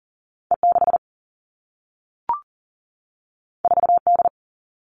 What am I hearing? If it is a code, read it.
Morse code, heard as a keyed tone, about 42 wpm, 700 Hz. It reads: E6 4D
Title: All Cut Numbers - 2-Digit 40wpm